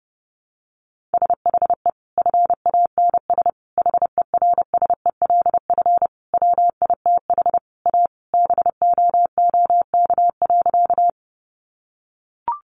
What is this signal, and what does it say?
Morse code, heard as a keyed tone, about 28 wpm, 700 Hz. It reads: SHE FANS HERSELF WITH A BOOK.
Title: Additional Easy Phrases 30wpm